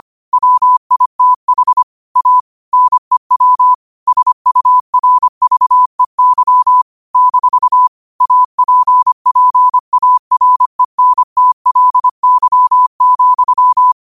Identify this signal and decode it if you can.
Morse code, heard as a keyed tone, about 25 wpm, 1000 Hz. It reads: WITH A NEW SURVEY - APPARENTLY,
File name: qotdfiles / 25wpm